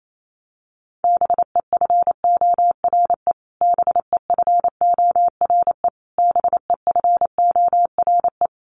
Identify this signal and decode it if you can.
Morse code, heard as a keyed tone, about 28 wpm, 700 Hz. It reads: BEFORE BEFORE BEFORE
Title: Top 300 Words Repeated 28wpm